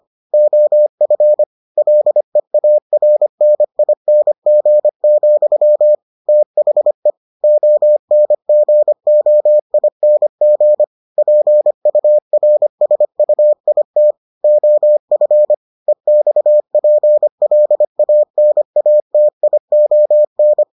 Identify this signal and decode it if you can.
Morse code, heard as a keyed tone, about 25 wpm, 600 Hz. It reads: OF LEARNING, THE ONGOING PURSUIT OF EXPLANATION